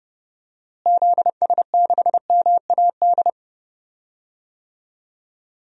Morse code, heard as a keyed tone, about 30 wpm, 700 Hz. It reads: ZS6MAD
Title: DXCC-100 Call Signs - 6 Characters - Repeated 2x - 30wpm